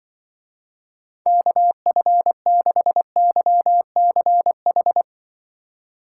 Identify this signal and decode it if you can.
Morse code, heard as a keyed tone, about 24 wpm, 700 Hz. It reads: KF6YCH
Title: Speed Racing - USA Call Signs 20wpm